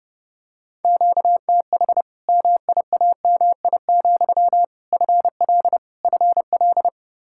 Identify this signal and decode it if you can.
Morse code, heard as a keyed tone, about 30 wpm, 700 Hz. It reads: QTH MIAMI, FL FL
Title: Practice QSO - Exchange 1 - Beginning - 30wpm